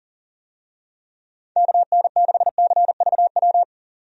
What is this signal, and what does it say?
Morse code, heard as a keyed tone, about 40 wpm, 700 Hz. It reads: KN6CVW
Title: USA Call Signs - 6 Characters - 40wpm